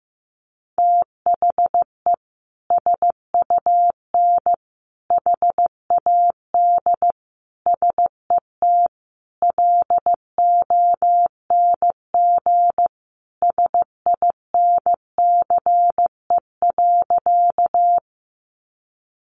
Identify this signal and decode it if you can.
Morse code, heard as a keyed tone, about 14 wpm, 700 Hz. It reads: THE SUN HAD SET LONG SINCE.
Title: Sentences from Top 300 Words 15wpm